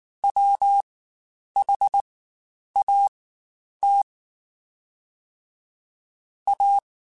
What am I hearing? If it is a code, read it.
Morse code, heard as a keyed tone, about 19 wpm, spaced as for 9 wpm, 800 Hz. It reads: WHAT A
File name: Quotes Spaced / Quotes Spaced-mp3s / Deceive sp @18